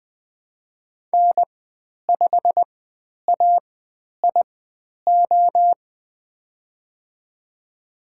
Morse code, heard as a keyed tone, about 20 wpm, spaced as for 10 wpm, 700 Hz. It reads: N5AIO
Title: 095 Lesson 16g - Letter c - Call Signs 20wpm